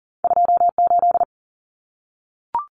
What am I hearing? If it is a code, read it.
Morse code, heard as a keyed tone, about 40 wpm, 700 Hz. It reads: 28
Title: Numbers - 2-Digit 40wpm